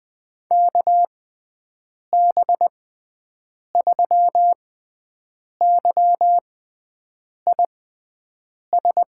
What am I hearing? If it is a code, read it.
Morse code, heard as a keyed tone, about 21 wpm, spaced as for 7 wpm, 700 Hz. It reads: KB3YIS
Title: USA Call Signs - 6 Characters - 6x Character Spacing - 20wpm